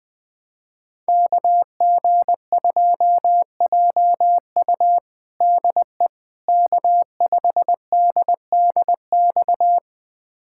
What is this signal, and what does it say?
Morse code, heard as a keyed tone, about 20 wpm, 700 Hz. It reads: KG2JU DE K5DDX
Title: USA Callsign DE Callsign - 5 Characters - 20wpm